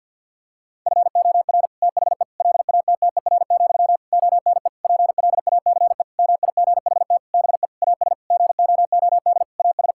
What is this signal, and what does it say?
Morse code, heard as a keyed tone, about 50 wpm, 700 Hz. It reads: FOR THE LATTER, ONE PLACE MIGHT BE AS GOOD AS
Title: Sentences from Top 1600 Words 50wpm